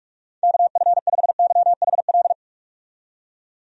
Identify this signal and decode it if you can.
Morse code, heard as a keyed tone, about 42 wpm, 700 Hz. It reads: KF5YHL
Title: USA Call Signs - 6 Characters - 45wpm